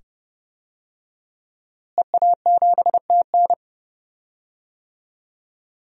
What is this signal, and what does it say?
Morse code, heard as a keyed tone, about 30 wpm, 700 Hz. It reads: EA7TN
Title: DXCC-100 Call Signs - 5 Characters - Repeated 2x - 30wpm